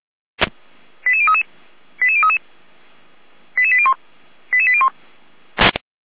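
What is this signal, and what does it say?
Five-tone selective call (ZVEI), heard as quick runs of five tones, tones of about 70 ms. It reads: 88033 88033 88801 88801
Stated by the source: ZVEI five-tone selective call received on an SDR.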